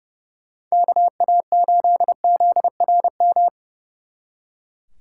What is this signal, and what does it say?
Morse code, heard as a keyed tone, about 40 wpm, 700 Hz. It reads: KA8ZRM